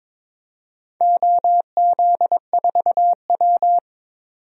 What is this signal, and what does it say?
Morse code, heard as a keyed tone, about 22 wpm, 700 Hz. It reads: OZ4W
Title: DXCC-100 Call Signs - 4 Characters - 22wpm